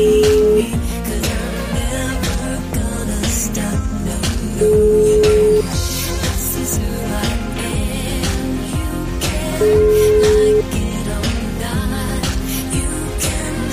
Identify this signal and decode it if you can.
Ringback tone (Europe, 425 Hz), with noise.